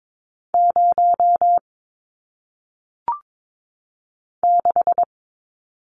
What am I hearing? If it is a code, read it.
Morse code, heard as a keyed tone, about 22 wpm, 700 Hz. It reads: 0 6
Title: Numbers - 1-Digit 22wpm